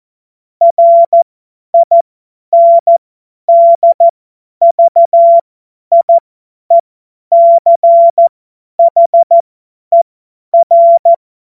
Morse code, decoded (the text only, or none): RINDVIECHER